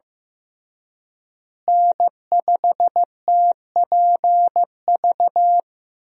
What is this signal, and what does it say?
Morse code, heard as a keyed tone, about 15 wpm, 700 Hz. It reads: N5TPV
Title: USA Call Signs - 5 Characters - Repeated 2x - 15wpm